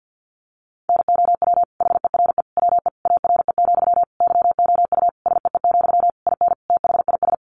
Morse code, heard as a keyed tone, about 50 wpm, 700 Hz. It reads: NOW HERE WE ARE, YOU SEE, IN THIS
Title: Sentences from Top 1200 Words 50wpm